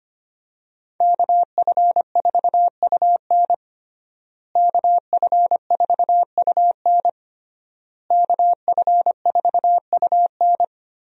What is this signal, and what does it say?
Morse code, heard as a keyed tone, about 25 wpm, 700 Hz. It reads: KF4UN KF4UN KF4UN
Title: USA Call Signs - 5 Characters - Repeated 3x - 25wpm